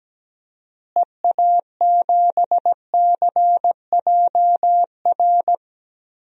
Morse code, heard as a keyed tone, about 17 wpm, 700 Hz. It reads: EA7CJR
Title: DXCC-100 Call Signs - 6 Characters - Repeated 3x - 17wpm